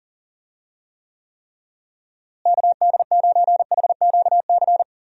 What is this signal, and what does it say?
Morse code, heard as a keyed tone, about 40 wpm, 700 Hz. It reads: KD9HQC
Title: USA Call Signs - 6 Characters - Repeated 3x - 40wpm